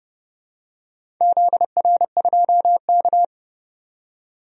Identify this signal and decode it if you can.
Morse code, heard as a keyed tone, about 30 wpm, 700 Hz. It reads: ZR2K
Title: DXCC-100 Call Signs - 4 Characters - 30wpm